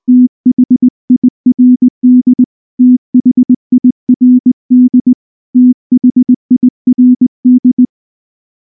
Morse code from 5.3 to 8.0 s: THIRD